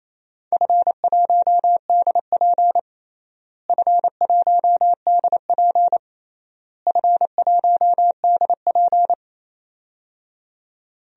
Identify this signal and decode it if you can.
Morse code, heard as a keyed tone, about 27 wpm, 700 Hz. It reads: F1DP F1DP F1DP
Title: DXCC-100 Call Signs - 4 Characters - Repeated 3x - 28wpm